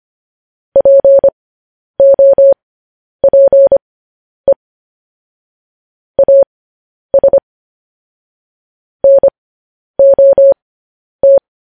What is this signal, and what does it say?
Morse code, heard as a keyed tone, about 25 wpm, spaced as for 10 wpm, 550 Hz. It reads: POPE AS NOT
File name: podcast / News headlines-1x-25x10